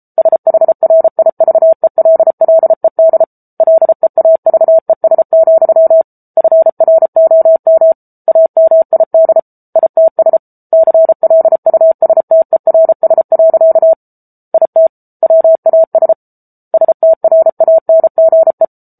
Morse code, heard as a keyed tone, about 34 wpm, 650 Hz. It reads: SHRIVELLED LEAVES, FROM AMID ITS CLUSTERS. IT WAS STRANGE
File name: Book Two / 07-Chapter 7 - 33 WPM